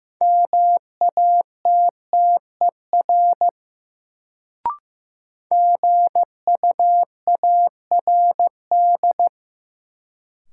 Morse code, heard as a keyed tone, about 15 wpm, 700 Hz. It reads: MATTER GUARD
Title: Sentences from Top 700 Words - New Word Drill - 15wpm